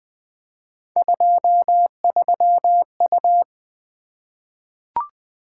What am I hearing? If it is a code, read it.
Morse code, heard as a keyed tone, about 21 wpm, 700 Hz. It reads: 23U
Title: Three Letter-Number Combinations - 20wpm